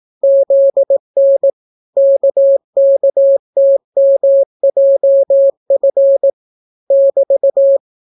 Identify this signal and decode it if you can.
Morse code, heard as a keyed tone, about 18 wpm, 550 Hz. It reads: ZN KKTMJF <BT>